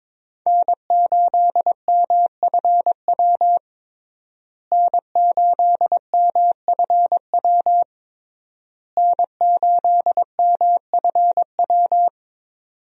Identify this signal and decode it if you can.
Morse code, heard as a keyed tone, about 22 wpm, 700 Hz. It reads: N8MFW N8MFW N8MFW
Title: USA Call Signs - 5 Characters - Repeated 3x - 22wpm